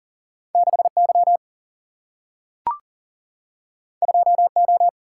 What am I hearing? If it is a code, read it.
Morse code, heard as a keyed tone, about 40 wpm, 700 Hz. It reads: 6Y 2O
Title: Single-Letter Single-Number 40wpm